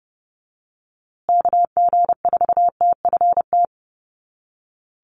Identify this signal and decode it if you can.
Morse code, heard as a keyed tone, about 30 wpm, 700 Hz. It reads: KG4TFT
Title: USA Call Signs Repeated 2x 30wpm